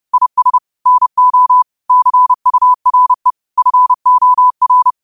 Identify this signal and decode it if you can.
Morse code, heard as a keyed tone, about 30 wpm, 1000 Hz. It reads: IS NO CURE FOR